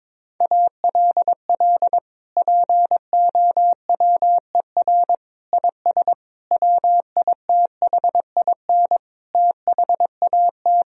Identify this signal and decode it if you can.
Morse code, heard as a keyed tone, about 22 wpm, 700 Hz. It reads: ALL POWER IS WITHIN THAT
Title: Sentences from Top 1000 Words - Review Part 4 - 22wpm